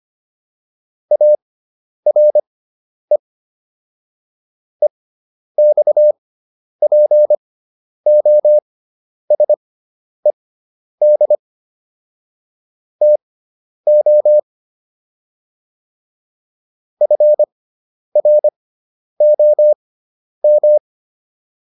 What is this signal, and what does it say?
Morse code, heard as a keyed tone, about 25 wpm, spaced as for 10 wpm, 600 Hz. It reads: ARE EXPOSED TO FROM